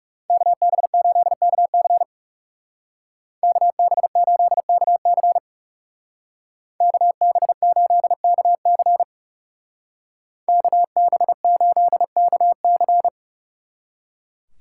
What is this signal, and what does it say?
Morse code, heard as a keyed tone, about 39 wpm, 700 Hz. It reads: KB8KC KB8KC KB8KC KB8KC